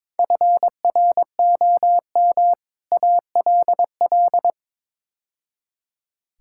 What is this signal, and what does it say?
Morse code, heard as a keyed tone, about 22 wpm, 700 Hz. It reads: FROM ALL
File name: practice / 22wpm / Build A Phrase - 5-Word Phrases - 22wpm